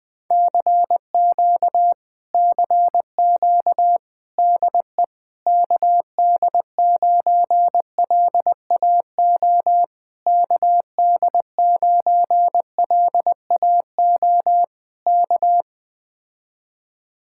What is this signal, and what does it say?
Morse code, heard as a keyed tone, about 20 wpm, 700 Hz. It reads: CQ CQ DE KD9LAO KD9LAO K
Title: Speed-Racing Common QSO Phrases - 17wpm